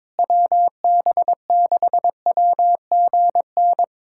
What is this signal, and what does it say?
Morse code, heard as a keyed tone, about 21 wpm, 700 Hz. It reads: WB6WGN
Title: USA Callsign DE Callsign - 6 Characters - 22wpm